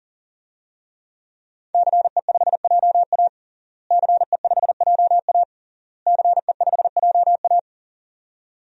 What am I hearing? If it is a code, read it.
Morse code, heard as a keyed tone, about 40 wpm, 700 Hz. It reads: CE5JA CE5JA CE5JA
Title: DXCC-100 Call Signs - 5 Characters - Repeated 3x - 40wpm